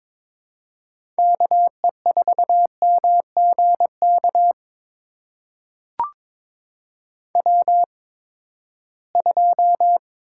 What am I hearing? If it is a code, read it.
Morse code, heard as a keyed tone, about 21 wpm, spaced as for 15 wpm, 700 Hz. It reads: KE4MGK W2
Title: USA Call Signs - 8x Character Spacing - 22wpm